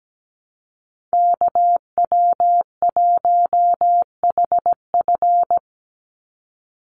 Morse code, heard as a keyed tone, about 17 wpm, 700 Hz. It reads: KW1HF
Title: USA Call Signs 17wpm